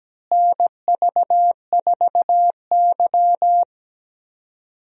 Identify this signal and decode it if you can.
Morse code, heard as a keyed tone, about 17 wpm, 700 Hz. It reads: NV4Y
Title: USA Call Signs - 4 Characters - Repeated 3x - 17wpm